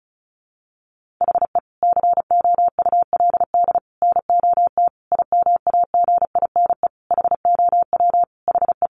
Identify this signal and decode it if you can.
Morse code, heard as a keyed tone, about 34 wpm, 700 Hz. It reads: HE COULD NOT IMAGINE HOW HE
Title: Sentences from Top 1400 Words 35wpm